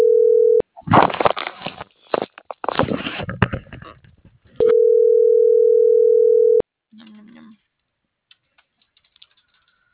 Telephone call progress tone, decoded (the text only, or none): ringback tone